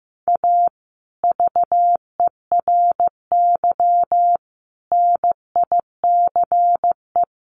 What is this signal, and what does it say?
Morse code, heard as a keyed tone, about 15 wpm, 700 Hz. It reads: A VERY NICE
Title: Sentences from Top 1200 Words 15wpm